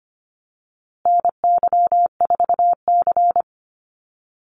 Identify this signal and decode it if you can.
Morse code, heard as a keyed tone, about 25 wpm, 700 Hz. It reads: NY4C
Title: USA Call Signs Repeated 3x 25wpm